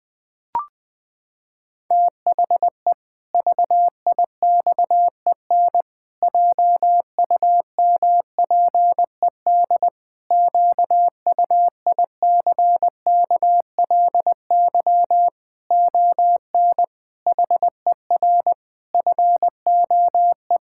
Morse code, heard as a keyed tone, about 19 wpm, 700 Hz. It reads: THE VIXEN JUMPED QUICKLY ON HER FOE
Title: Pangram Challenge v2 - 20wpm